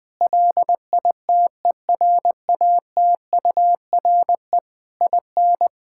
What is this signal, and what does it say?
Morse code, heard as a keyed tone, about 20 wpm, 700 Hz. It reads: LITERATURE IN
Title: Sentences from Top 1000 Words - Review Part 6 - 20wpm